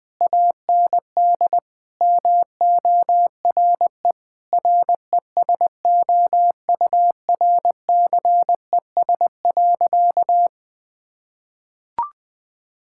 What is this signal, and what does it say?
Morse code, heard as a keyed tone, about 20 wpm, 700 Hz. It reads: AND MORE RESOURCES.
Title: Sentences with 2-Word Phrases - Spoken Beginning - 20wpm